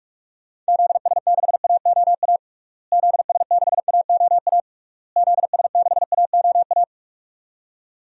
Morse code, heard as a keyed tone, about 46 wpm, 700 Hz. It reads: ZS6AOA ZS6AOA ZS6AOA
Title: DXCC-100 Call Signs - 6 Characters - Repeated 3x - 45wpm